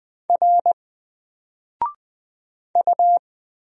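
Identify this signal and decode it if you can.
Morse code, heard as a keyed tone, about 20 wpm, 700 Hz. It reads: R U
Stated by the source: Single letter and number Morse drill at 20 wpm, from N4LCD.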